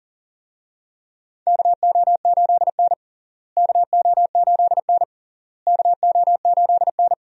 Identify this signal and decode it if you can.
Morse code, heard as a keyed tone, about 40 wpm, 700 Hz. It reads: KO8N KO8N KO8N
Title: USA Call Signs - 4 Characters - Repeated 3x - 40wpm